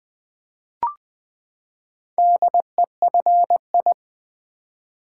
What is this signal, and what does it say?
Morse code, heard as a keyed tone, about 20 wpm, 700 Hz. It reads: DEFI
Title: Build A Word - 4 Letters At A Time - 10-Letter Words - 20wpm